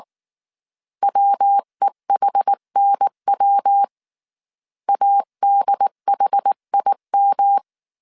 Morse code, heard as a keyed tone, about 19 wpm, 800 Hz. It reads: WEHNW ADHIM
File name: CW / DET / 1 3 1500C S B 18wpm 1916